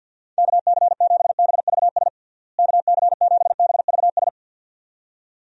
Morse code, heard as a keyed tone, about 52 wpm, 700 Hz. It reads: KC7BVS KC7BVS
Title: USA Call Signs - 6 Characters - Repeated 2x - 50wpm